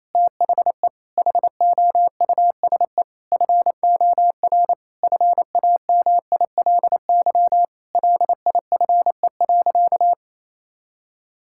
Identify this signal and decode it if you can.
Morse code, heard as a keyed tone, about 27 wpm, 700 Hz. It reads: THE HOUSE FOR FAMILY LIFE.
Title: Sentences from Top 1000 Words - Review Part 5 - 28wpm